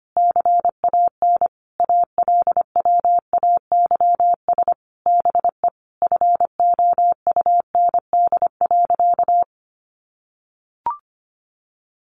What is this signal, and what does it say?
Morse code, heard as a keyed tone, about 25 wpm, 700 Hz. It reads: CAN ALWAYS BE FOUND.
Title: Sentences from Top 1300 Words 25wpm